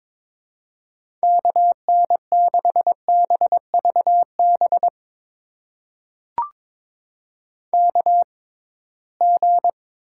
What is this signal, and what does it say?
Morse code, heard as a keyed tone, about 22 wpm, spaced as for 17 wpm, 700 Hz. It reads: KN6BVB KG
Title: USA Call Signs - 6 Characters - 6x Character Spacing - 22wpm